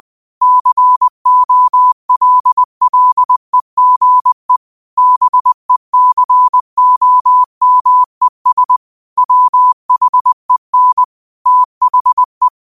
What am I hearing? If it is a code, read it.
Morse code, heard as a keyed tone, about 20 wpm, 1000 Hz. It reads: COLLEGE BECOMES WHEN THE